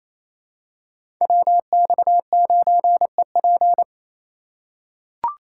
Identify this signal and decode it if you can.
Morse code, heard as a keyed tone, about 28 wpm, 700 Hz. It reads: WX9EP
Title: USA Call Signs - 5 Characters - 4x Character Spacing - 28wpm